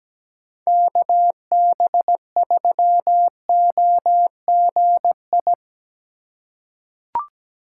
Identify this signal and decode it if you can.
Morse code, heard as a keyed tone, about 17 wpm, 700 Hz. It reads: KB3OGI
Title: USA Call Signs - 6 Characters - 2x Character Spacing - 17wpm